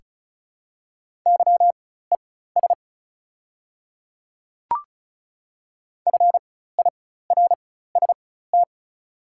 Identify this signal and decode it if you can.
Morse code, heard as a keyed tone, about 35 wpm, spaced as for 16 wpm, 690 Hz. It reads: YES FIRST
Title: 4x Character Spacing - Top 500 Words - 35wpm